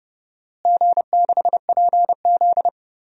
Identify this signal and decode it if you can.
Morse code, heard as a keyed tone, about 32 wpm, 700 Hz. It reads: G6PZ
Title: DXCC-100 Call Signs - 4 Characters - Repeated 3x - 30wpm